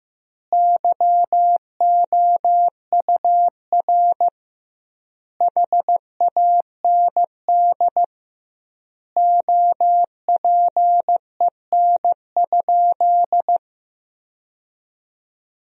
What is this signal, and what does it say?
Morse code, heard as a keyed tone, about 16 wpm, 700 Hz. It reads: YOUR HAND OPEN?
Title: 2x Word-Spacing - Sentences from Top 300 Words - 15wpm